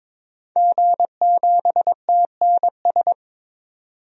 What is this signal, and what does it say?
Morse code, heard as a keyed tone, about 22 wpm, 700 Hz. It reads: G7TNS
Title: DXCC-100 Call Signs - 5 Characters - 22wpm